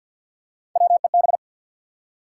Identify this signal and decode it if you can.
Morse code, heard as a keyed tone, about 53 wpm, 700 Hz. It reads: WEB
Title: Practice QSO - Exchange 3 - More Info - 50wpm